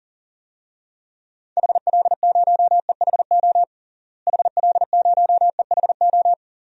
Morse code, heard as a keyed tone, about 37 wpm, 700 Hz. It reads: HL0EHO HL0EHO
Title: DXCC-100 Call Signs - 6 Characters - Repeated 3x - 40wpm